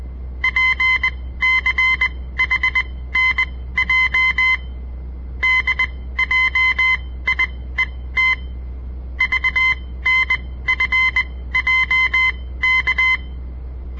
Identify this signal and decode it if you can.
Morse code, heard as a keyed tone, about 20 wpm, spaced as for 15 wpm, 2050 Hz. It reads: PCHNJ DJIET VNFJK
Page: On the air Morse code practice from the Southern Montana Amatuer Radio Association; all code, no read back